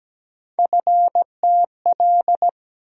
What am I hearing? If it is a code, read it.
Morse code, heard as a keyed tone, about 17 wpm, 700 Hz. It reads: FTL
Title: USA Call Sign Postfixes - 3 Letters 17wpm